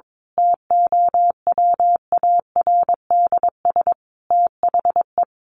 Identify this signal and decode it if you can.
Morse code, heard as a keyed tone, about 22 wpm, 700 Hz. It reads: TOWARDS THE
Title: Sentences from Top 1100 Words 22wpm